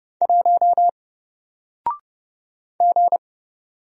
Morse code, heard as a keyed tone, about 30 wpm, 700 Hz. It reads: 1 G
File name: Single-Letter-Number-30wpm